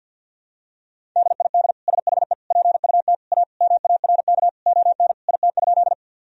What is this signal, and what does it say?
Morse code, heard as a keyed tone, about 46 wpm, 700 Hz. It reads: DID SHE PUT A MARK ON IT?
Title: Sentences from Top 1000 Words - Review Part 1 - 50wpm